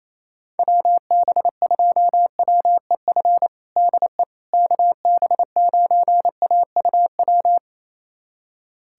Morse code, heard as a keyed tone, about 27 wpm, 700 Hz. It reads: WB2WEF DE KB9AUW
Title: USA Callsign DE Callsign 28wpm